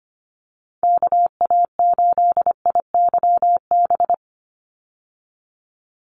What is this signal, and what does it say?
Morse code, heard as a keyed tone, about 25 wpm, 700 Hz. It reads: KA8IYB
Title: USA Call Signs 25wpm